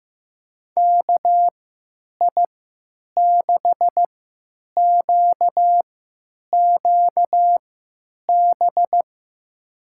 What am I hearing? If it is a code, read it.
Morse code, heard as a keyed tone, about 15 wpm, spaced as for 9 wpm, 700 Hz. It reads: KI6QQB
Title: USA Call Signs - 6 Characters - 3x Character Spacing - 15wpm